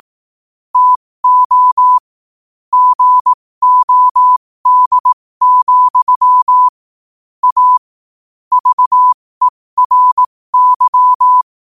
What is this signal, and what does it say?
Morse code, heard as a keyed tone, about 18 wpm, spaced as for 15 wpm, 1000 Hz. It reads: TO GOD, A VERY
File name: qotdfiles / 15wpm